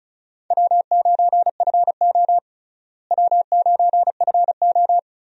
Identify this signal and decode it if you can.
Morse code, heard as a keyed tone, about 36 wpm, 700 Hz. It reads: W9FO W9FO
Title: USA Call Signs - 4 Characters - Repeated 3x - 35wpm